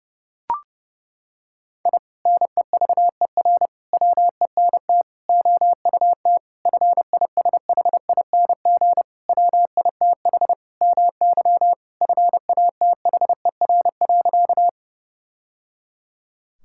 Morse code, heard as a keyed tone, about 32 wpm, 700 Hz. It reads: I NEVER WENT OUT FISHING WITH MY FATHER.
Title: Sentences from Top 1000 Words - Review Part 5 - 30wpm